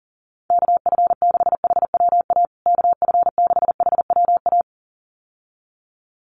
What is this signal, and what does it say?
Morse code, heard as a keyed tone, about 40 wpm, 700 Hz. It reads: KF6HWA KF6HWA
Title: USA Call Signs Repeated 2x 40wpm